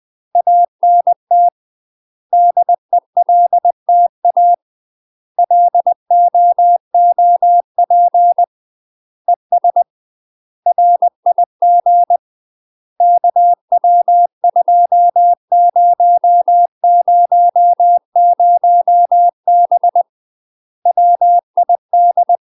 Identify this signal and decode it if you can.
Morse code, heard as a keyed tone, about 20 wpm, 700 Hz. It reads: ANT DELTA LOOP ES RIG KW2000B WID
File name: UK MASTER QSOs TRAINING / 20WPM / UK MASTER QSOs 0028